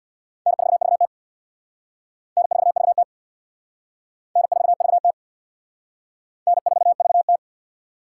Morse code, heard as a keyed tone, about 61 wpm, 700 Hz. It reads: N4VT N4VT N4VT N4VT
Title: Speed Racing - USA 1x2 Call Signs 50wpm